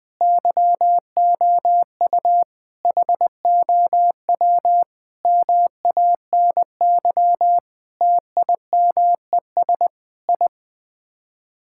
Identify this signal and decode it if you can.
Morse code, heard as a keyed tone, about 21 wpm, 700 Hz. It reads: YOU HOW MANY TIMES I
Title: Sets of 8 Words - 20wpm